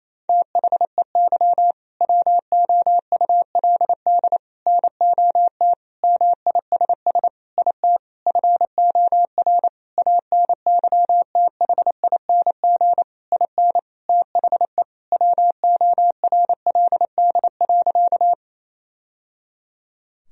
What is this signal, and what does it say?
Morse code, heard as a keyed tone, about 28 wpm, 700 Hz. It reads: THEY WOULD NOT MISS IT FOR ANYTHING IN THE WORLD.
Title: Sentences from Top 1000 Words - Review Part 3 - 28wpm